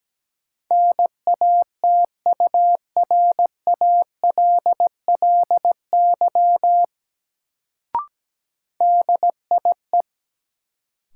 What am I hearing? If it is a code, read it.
Morse code, heard as a keyed tone, about 17 wpm, 700 Hz. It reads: NATURALLY DIE
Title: Sentences from Top 1300 Words - New Word Drill - 17wpm